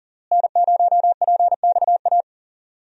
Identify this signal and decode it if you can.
Morse code, heard as a keyed tone, about 40 wpm, 700 Hz. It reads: N0PXA
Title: USA Call Signs - 5 Characters - Repeated 3x - 40wpm